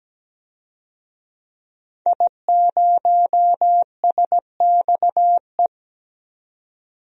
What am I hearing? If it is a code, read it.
Morse code, heard as a keyed tone, about 17 wpm, 700 Hz. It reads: I0SXE
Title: DXCC-100 Call Signs - 5 Characters - 17wpm